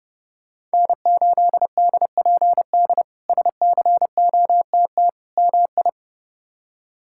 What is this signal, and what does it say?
Morse code, heard as a keyed tone, about 30 wpm, 700 Hz. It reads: N8DPD SCOTT MI
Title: K1USN SST QSO Practice - 30wpm